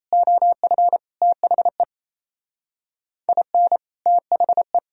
Morse code, heard as a keyed tone, about 36 wpm, 700 Hz. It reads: OF THE IN THE